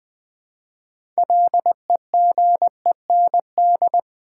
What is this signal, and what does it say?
Morse code, heard as a keyed tone, about 20 wpm, 700 Hz. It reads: LEGEND
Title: Build A Word - 2 Letters At A Time - 6-Letter Words - 20wpm